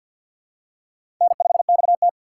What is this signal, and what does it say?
Morse code, heard as a keyed tone, about 47 wpm, 700 Hz. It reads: N5XT
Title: USA Call Signs - 4 Characters - 6x Character Spacing - 50wpm